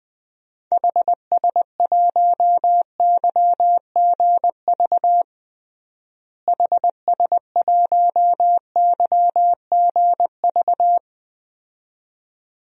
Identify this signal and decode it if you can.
Morse code, heard as a keyed tone, about 20 wpm, 700 Hz. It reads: HS1YGV HS1YGV
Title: DXCC-100 Call Signs - 6 Characters - Repeated 2x - 20wpm